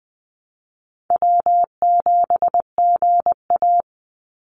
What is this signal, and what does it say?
Morse code, heard as a keyed tone, about 20 wpm, 700 Hz. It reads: W7GA
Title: NA SOTA Chaser Callsigns 20wpm